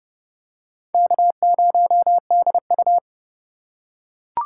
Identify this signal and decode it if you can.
Morse code, heard as a keyed tone, about 27 wpm, 700 Hz. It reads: K0DU